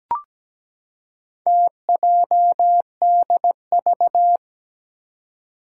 Morse code, heard as a keyed tone, about 17 wpm, 700 Hz. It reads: TJDV
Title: Four Challenging Letter-Number Combinations 17wpm